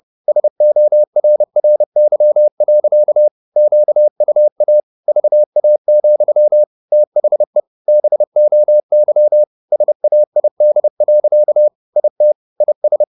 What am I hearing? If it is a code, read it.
Morse code, heard as a keyed tone, about 30 wpm, 600 Hz. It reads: SORRY. QUA VA, THE BOY SAID. IT IS
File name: the old man and the sea / the old man and the sea 1 30wpm